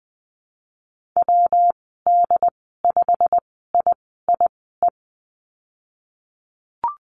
Morse code, heard as a keyed tone, about 19 wpm, spaced as for 14 wpm, 700 Hz. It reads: WD5IIE
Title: 128 Lesson 21g - Number 6 - Call Signs 20wpm